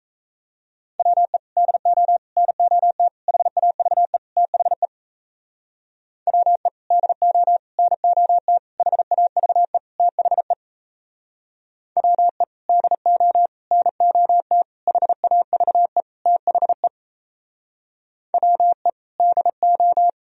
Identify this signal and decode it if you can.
Morse code, heard as a keyed tone, about 35 wpm, 700 Hz. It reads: WE DO NOT HAVE THE WE DO NOT HAVE THE WE DO NOT HAVE THE WE DO
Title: Speed-Racing - Sets of 5 Words 28wpm